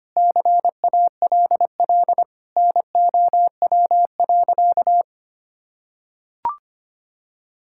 Morse code, heard as a keyed tone, about 25 wpm, 700 Hz. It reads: CALL NOW.
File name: practice / 25wpm / Sentences with 4-Word Phrases - 25wpm